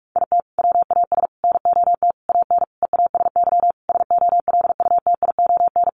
Morse code, heard as a keyed tone, about 44 wpm, 700 Hz. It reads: IT WAS NOT AN EASY SOLUTION
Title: Sentences from Top 1300 Words 45wpm